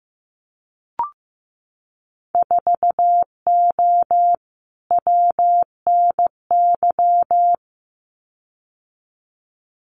Morse code, heard as a keyed tone, about 15 wpm, 700 Hz. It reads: 4O WNY
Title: Winter Field Day Exchanges Encore 15wpm